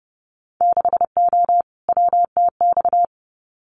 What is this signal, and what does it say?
Morse code, heard as a keyed tone, about 30 wpm, 700 Hz. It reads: 6O WTX
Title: Winter Field Day Exchanges Encore 30wpm